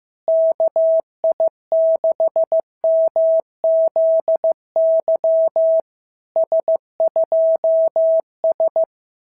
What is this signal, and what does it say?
Morse code, heard as a keyed tone, about 15 wpm, 650 Hz. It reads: KI6MZY S2S
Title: SOTA QSO Practice 15wpm